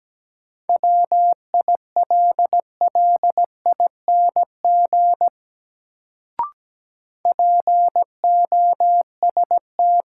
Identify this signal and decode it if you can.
Morse code, heard as a keyed tone, about 17 wpm, 700 Hz. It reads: WILLING POST
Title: Sentences from Top 1000 Words - New Word Drill - 17wpm